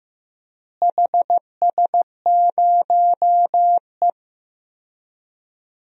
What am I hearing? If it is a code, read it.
Morse code, heard as a keyed tone, about 15 wpm, 700 Hz. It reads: HS0E
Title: DXCC-100 Call Signs - 4 Characters - Repeated 3x - 15wpm